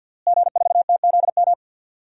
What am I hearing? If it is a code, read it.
Morse code, heard as a keyed tone, about 50 wpm, 700 Hz. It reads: G4TZM
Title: DXCC-100 Call Signs - 5 Characters - Repeated 2x - 50wpm